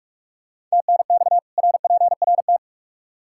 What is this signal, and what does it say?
Morse code, heard as a keyed tone, about 46 wpm, 700 Hz. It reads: TNX RPRT
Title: Practice QSO - Exchange 2 - Info - 45wpm